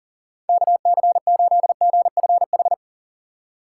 Morse code, heard as a keyed tone, about 54 wpm, spaced as for 47 wpm, 700 Hz. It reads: KC8GFH
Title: Speed Racing - USA 2x3 Call Signs 40wpm